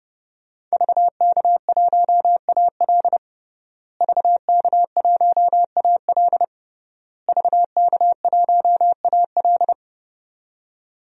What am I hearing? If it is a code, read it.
Morse code, heard as a keyed tone, about 30 wpm, 700 Hz. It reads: VK1AL VK1AL VK1AL
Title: DXCC-100 Call Signs - 5 Characters - Repeated 3x - 30wpm